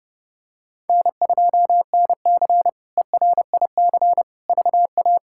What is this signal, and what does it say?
Morse code, heard as a keyed tone, about 32 wpm, 700 Hz. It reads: N2NC ERIC VA
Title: K1USN SST QSO Practice - 30wpm